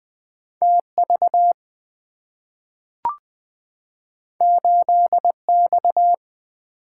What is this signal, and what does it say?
Morse code, heard as a keyed tone, about 20 wpm, 700 Hz. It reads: TV 8X
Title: Two Letter-Number Combinations - 20wpm